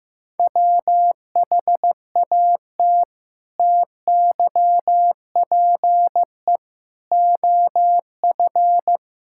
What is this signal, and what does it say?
Morse code, heard as a keyed tone, about 15 wpm, 700 Hz. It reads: WHAT TYPE OF
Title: Sentences from Top 1000 Words - Review Part 6 - 15wpm